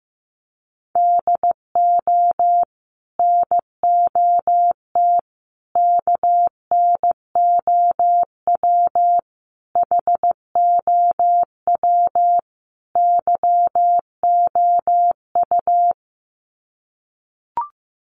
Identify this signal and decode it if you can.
Morse code, heard as a keyed tone, about 15 wpm, 700 Hz. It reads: DO NOT KNOW HOW YOU
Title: Sets of 5 Words Encore 15wpm